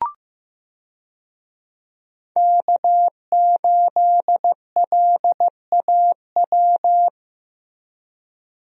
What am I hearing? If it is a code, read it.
Morse code, heard as a keyed tone, about 15 wpm, 700 Hz. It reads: K8LAW
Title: USA Call Signs - 5 Characters - Repeated 3x - 15wpm